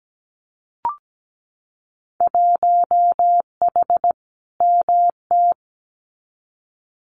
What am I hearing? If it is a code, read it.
Morse code, heard as a keyed tone, about 17 wpm, 700 Hz. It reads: 1H MT
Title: Winter Field Day Exchanges Encore 17wpm